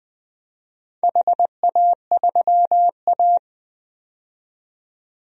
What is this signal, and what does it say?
Morse code, heard as a keyed tone, about 20 wpm, 700 Hz. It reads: HA3A
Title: DXCC-100 Call Signs - 4 Characters - 20wpm